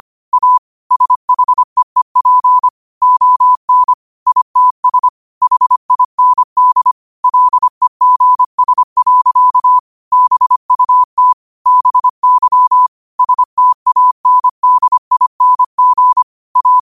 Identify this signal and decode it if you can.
Morse code, heard as a keyed tone, about 25 wpm, 1000 Hz. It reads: A SHEEP ON ITS HIND LEGS. BUT BY STANDING A
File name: qotdfiles / 25wpm